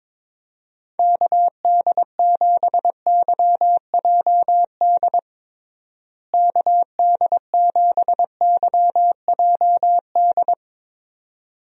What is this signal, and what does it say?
Morse code, heard as a keyed tone, about 22 wpm, 700 Hz. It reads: KD7YJD KD7YJD
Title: USA Call Signs - 6 Characters - Repeated 2x - 22wpm